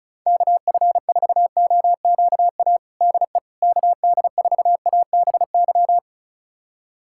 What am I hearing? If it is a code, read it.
Morse code, heard as a keyed tone, about 34 wpm, 700 Hz. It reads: KF4OQA DE KD4ABY
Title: USA Callsign DE Callsign 35wpm